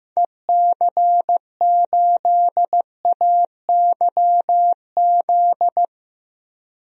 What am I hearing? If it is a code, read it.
Morse code, heard as a keyed tone, about 15 wpm, 700 Hz. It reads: EC8AYZ
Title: DXCC-100 Call Signs - 6 Characters - Repeated 3x - 15wpm